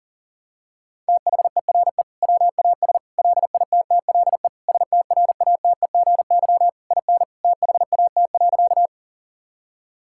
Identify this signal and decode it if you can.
Morse code, heard as a keyed tone, about 40 wpm, 700 Hz. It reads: THERE WAS LITTLE STRATEGY IN THAT.
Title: Sentences from Top 1000 Words - Review Part 5 - 40wpm